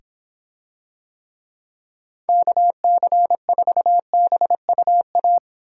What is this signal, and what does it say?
Morse code, heard as a keyed tone, about 25 wpm, 700 Hz. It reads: KC4BUA